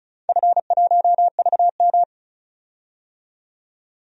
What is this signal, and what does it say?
Morse code, heard as a keyed tone, about 36 wpm, 700 Hz. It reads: F1VM
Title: DXCC-100 Call Signs - 4 Characters - 35wpm